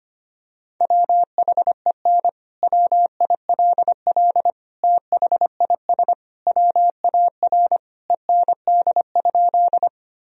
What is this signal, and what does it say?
Morse code, heard as a keyed tone, about 25 wpm, 700 Hz. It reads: WHEN WILL THIS WAR END?
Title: Sentences from Top 1000 Words - Review Part 2 - 25wpm